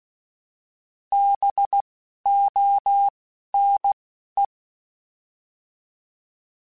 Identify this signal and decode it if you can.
Morse code, heard as a keyed tone, about 16 wpm, 800 Hz. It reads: B O N E